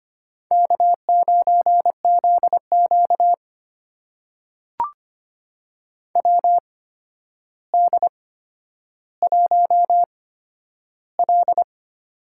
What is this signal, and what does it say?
Morse code, heard as a keyed tone, about 25 wpm, spaced as for 11 wpm, 700 Hz. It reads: K9ZQ WD1L
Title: USA Call Signs - 4 Characters - 8x Character Spacing - 25wpm